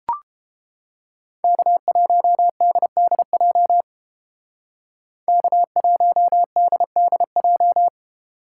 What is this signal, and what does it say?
Morse code, heard as a keyed tone, about 29 wpm, 700 Hz. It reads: K1DDJ K1DDJ